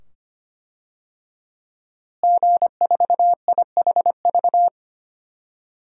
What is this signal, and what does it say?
Morse code, heard as a keyed tone, about 25 wpm, 700 Hz. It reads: G4IHV
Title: DXCC-100 Call Signs - 5 Characters - Repeated 2x - 25wpm